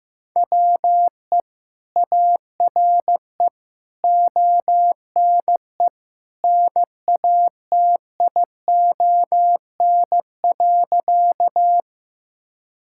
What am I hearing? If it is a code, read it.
Morse code, heard as a keyed tone, about 15 wpm, 700 Hz. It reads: WE ARE ONE NATION.
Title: Sentences from Top 1000 Words - Review Part 6 - 15wpm